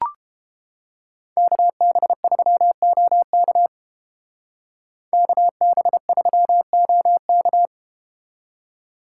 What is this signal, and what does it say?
Morse code, heard as a keyed tone, about 33 wpm, 700 Hz. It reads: KB3OK KB3OK